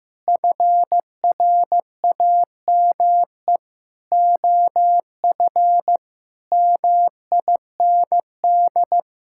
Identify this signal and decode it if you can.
Morse code, heard as a keyed tone, about 15 wpm, 700 Hz. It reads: FRAME OF MIND
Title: Sentences from Top 1000 Words - Review Part 6 - 15wpm